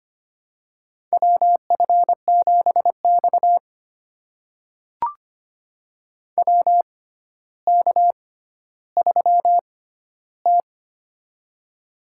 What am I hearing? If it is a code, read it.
Morse code, heard as a keyed tone, about 25 wpm, spaced as for 13 wpm, 700 Hz. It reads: WF7X WK3T
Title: USA Call Signs - 4 Characters - 6x Character Spacing - 25wpm